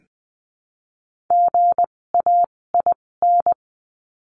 Morse code, heard as a keyed tone, about 20 wpm, spaced as for 16 wpm, 700 Hz. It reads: GAIN
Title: 256 Lesson 42a - Words 12-20wpm